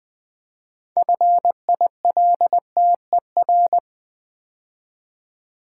Morse code, heard as a keyed tone, about 20 wpm, 700 Hz. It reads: FILTER
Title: Build A Word - 2 Letters At A Time - 6-Letter Words - 20wpm